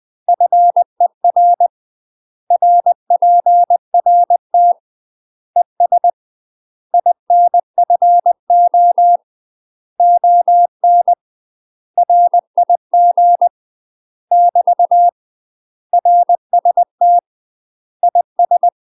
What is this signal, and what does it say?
Morse code, heard as a keyed tone, about 20 wpm, 700 Hz. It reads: FER RPRT ES INFO ON RIG <BT> RST IS
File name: UK MASTER QSOs TRAINING / 20WPM / UK MASTER QSOs 0063